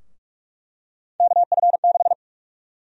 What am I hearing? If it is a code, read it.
Morse code, heard as a keyed tone, about 43 wpm, 700 Hz. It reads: KF6
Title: USA Call Sign Prefixes - Repeated 2x - 45wpm